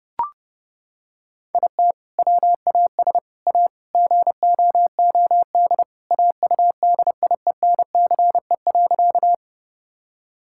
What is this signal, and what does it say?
Morse code, heard as a keyed tone, about 30 wpm, 700 Hz. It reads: IT WAS A GOOD AUDIENCE.
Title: Sentences from Top 1000 Words - Review Part 5 - 30wpm